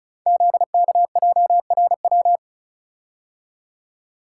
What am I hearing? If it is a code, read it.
Morse code, heard as a keyed tone, about 36 wpm, 690 Hz. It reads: ZKJRW
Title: Code Groups - 5 Letter:Numbers - 35wpm